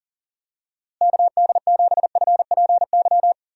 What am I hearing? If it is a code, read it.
Morse code, heard as a keyed tone, about 40 wpm, 700 Hz. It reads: KD7FPY
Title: USA Call Signs - 6 Characters - Repeated 3x - 40wpm